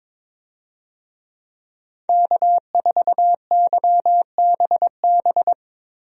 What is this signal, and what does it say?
Morse code, heard as a keyed tone, about 22 wpm, 700 Hz. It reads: K4YBB
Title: USA Call Signs - 5 Characters - Repeated 2x - 22wpm